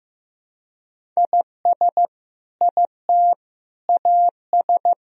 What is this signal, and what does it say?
Morse code, heard as a keyed tone, about 15 wpm, 700 Hz. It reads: IS IT AS
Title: Sentences from Top 200 Words and Abbreviations 15wpm